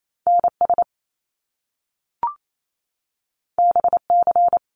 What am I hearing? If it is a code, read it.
Morse code, heard as a keyed tone, about 28 wpm, 700 Hz. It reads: NS BC
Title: Canadian Provinces and Territories 28wpm